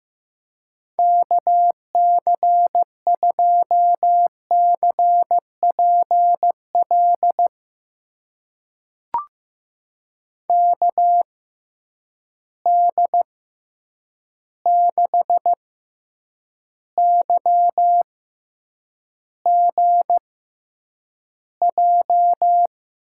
Morse code, heard as a keyed tone, about 15 wpm, spaced as for 8 wpm, 700 Hz. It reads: KC2CPL KD6YGJ
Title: USA Call Signs - 6 Characters - 6x Character Spacing - 15wpm